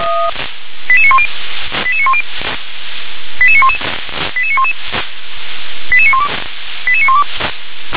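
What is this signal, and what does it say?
Five-tone selective call (ZVEI), heard as quick runs of five tones, tones of about 70 ms. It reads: 88910 88910 88911 88911 88912 88912
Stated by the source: ZVEI five-tone call 88952 received by SDR.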